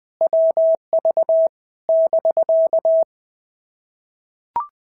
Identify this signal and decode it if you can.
Morse code, heard as a keyed tone, about 20 wpm, 650 Hz. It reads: WV <BK>
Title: POTA QSO Practice - 20wpm